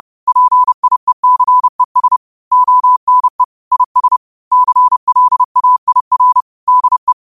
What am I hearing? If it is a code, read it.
Morse code, heard as a keyed tone, about 30 wpm, 1000 Hz. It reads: PIECES ONE IS CLAIR DE
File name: qotdfiles / 30wpm